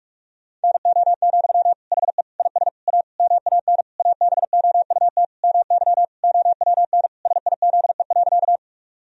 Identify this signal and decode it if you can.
Morse code, heard as a keyed tone, about 41 wpm, 700 Hz. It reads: NO, HE IS A MAN ABOUT MY OWN SIZE.
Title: Sentences from Top 1000 Words - Review Part 2 - 45wpm